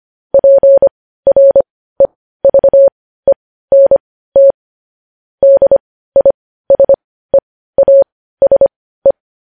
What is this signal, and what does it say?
Morse code, heard as a keyed tone, about 25 wpm, spaced as for 15 wpm, 550 Hz. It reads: PREVENT DISEASE
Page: news headlines in morse code at 15 wpm